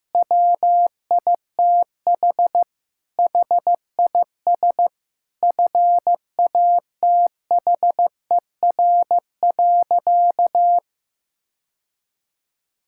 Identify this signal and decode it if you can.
Morse code, heard as a keyed tone, about 15 wpm, 700 Hz. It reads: WITH HIS FATHER.
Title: Sentences from Top 1000 Words - Review Part 5 - 15wpm